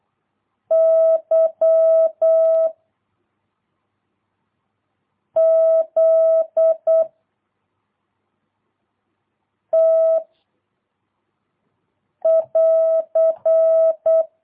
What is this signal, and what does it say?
Morse code, heard as a keyed tone, about 8 wpm, 660 Hz. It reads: Y Z T <AR>
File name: jh8cbh / 2022cw